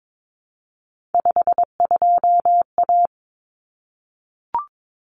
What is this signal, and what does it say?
Morse code, heard as a keyed tone, about 22 wpm, 700 Hz. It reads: 52A
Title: Common Cut Numbers - 3-Digit 22wpm